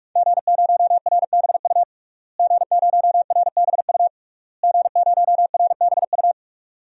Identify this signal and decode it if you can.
Morse code, heard as a keyed tone, about 46 wpm, 700 Hz. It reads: G0RBU G0RBU G0RBU
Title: DXCC-100 Call Signs - 5 Characters - Repeated 3x - 45wpm